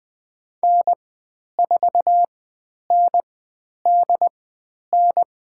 Morse code, heard as a keyed tone, about 20 wpm, spaced as for 10 wpm, 700 Hz. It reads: N4NDN
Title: 102 Lesson 17g - Letter u - Call Signs 20wpm